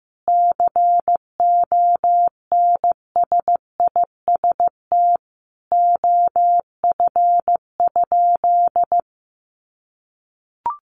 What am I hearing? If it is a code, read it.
Morse code, heard as a keyed tone, about 15 wpm, 700 Hz. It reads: CONSIST OF?
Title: Sentences from Top 1400 Words 15wpm